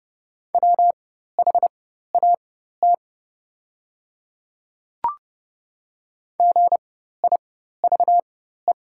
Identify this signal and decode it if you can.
Morse code, heard as a keyed tone, about 30 wpm, spaced as for 14 wpm, 700 Hz. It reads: WHAT GIVE
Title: 4x Character Spacing - Top 300 Words - 30wpm